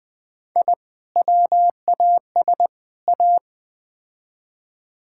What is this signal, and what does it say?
Morse code, heard as a keyed tone, about 20 wpm, 700 Hz. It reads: I WAS A
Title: Build A Phrase - 3-Word Phrases - 20wpm